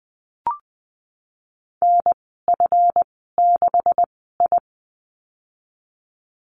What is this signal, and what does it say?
Morse code, heard as a keyed tone, about 19 wpm, spaced as for 14 wpm, 700 Hz. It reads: NF6I
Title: 161 Lesson 26g - Letter g - Call Signs 20wpm